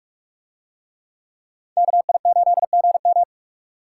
Morse code, heard as a keyed tone, about 42 wpm, 700 Hz. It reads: KI8GM